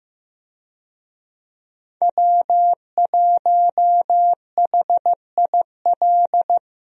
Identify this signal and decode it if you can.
Morse code, heard as a keyed tone, about 15 wpm, 700 Hz. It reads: W1HIL
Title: USA Call Signs - 5 Characters - 15wpm